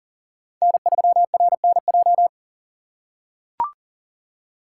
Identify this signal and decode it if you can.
Morse code, heard as a keyed tone, about 40 wpm, 700 Hz. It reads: N3RNJ
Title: USA Call Signs - 5 Characters - 4x Character Spacing - 40wpm